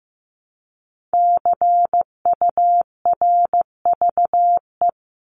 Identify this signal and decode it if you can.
Morse code, heard as a keyed tone, about 15 wpm, 700 Hz. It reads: CURVE
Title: Top 1501-1750 Words 15wpm